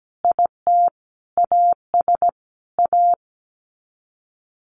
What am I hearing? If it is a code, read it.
Morse code, heard as a keyed tone, about 17 wpm, 700 Hz. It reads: IT AS A
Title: Sets of 5 Words Encore2 17wpm